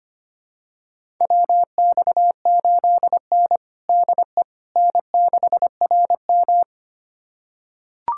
Morse code, heard as a keyed tone, about 25 wpm, 700 Hz. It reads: WX8N DE N6RM
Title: USA Callsign DE Callsign - 4 Characters - 25wpm